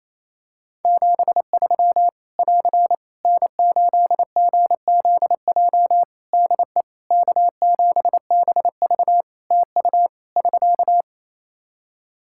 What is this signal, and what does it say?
Morse code, heard as a keyed tone, about 28 wpm, 700 Hz. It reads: 73 <AR> N8GZJ DE K7BV TU <SK>
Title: Practice QSO - Exchange 4 - Ending - 28wpm